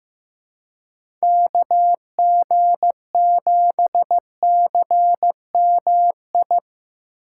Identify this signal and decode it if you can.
Morse code, heard as a keyed tone, about 15 wpm, 700 Hz. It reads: KG7CMI
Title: USA Call Signs - 3x Character Spacing - 15wpm